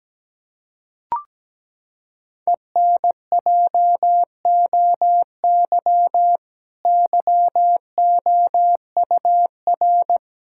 Morse code, heard as a keyed tone, about 17 wpm, 700 Hz. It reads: ENJOY YOUR
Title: Sentences from Top 1000 Words - Review Part 5 - 17wpm